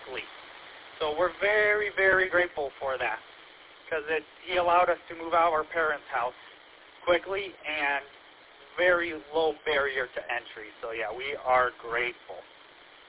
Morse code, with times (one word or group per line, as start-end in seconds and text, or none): none